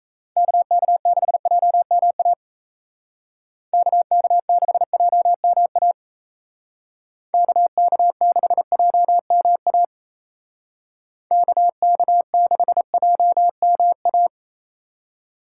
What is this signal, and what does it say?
Morse code, heard as a keyed tone, about 36 wpm, 700 Hz. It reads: KK6JMA KK6JMA KK6JMA KK6JMA